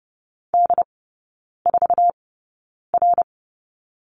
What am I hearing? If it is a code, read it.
Morse code, heard as a keyed tone, about 30 wpm, spaced as for 9 wpm, 700 Hz. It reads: D4R
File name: Lesson 18e - Letter m - Three Characters 30wpm